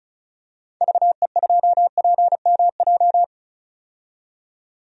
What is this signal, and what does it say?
Morse code, heard as a keyed tone, about 34 wpm, 700 Hz. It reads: VE2PMJ
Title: DXCC-100 Call Signs - 6 Characters - 35wpm